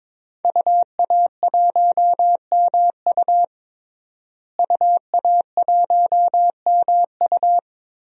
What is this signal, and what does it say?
Morse code, heard as a keyed tone, about 22 wpm, 700 Hz. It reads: UA1MU UA1MU
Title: DXCC-100 Call Signs - 5 Characters - Repeated 3x - 22wpm